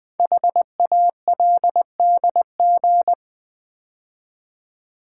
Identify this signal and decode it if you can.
Morse code, heard as a keyed tone, about 20 wpm, 700 Hz. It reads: HALDG
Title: Code Groups - 5 Letters - 20wpm